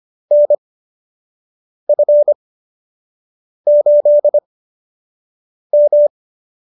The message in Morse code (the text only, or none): NF8M